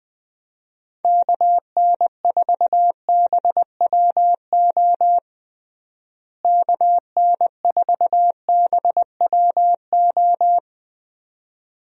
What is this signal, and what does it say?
Morse code, heard as a keyed tone, about 20 wpm, 700 Hz. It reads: KN4BWO KN4BWO
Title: USA Call Signs - 6 Characters - Repeated 2x - 20wpm